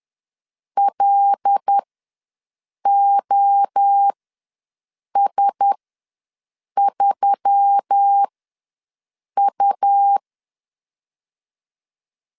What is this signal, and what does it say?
Morse code, heard as a keyed tone, about 10 wpm, spaced as for 6 wpm, 800 Hz. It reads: LOS3U
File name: LEZIONI / 5 LBVC / 5 LBVC ws 10wpm 3733